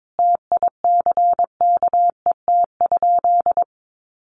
Morse code, heard as a keyed tone, about 22 wpm, 700 Hz. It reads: TICKET?
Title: Sentences from Top 1400 Words 22wpm